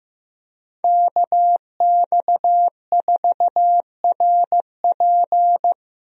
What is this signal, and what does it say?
Morse code, heard as a keyed tone, about 15 wpm, 700 Hz. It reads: KX4RP
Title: USA Call Signs - 5 Characters - Repeated 2x - 15wpm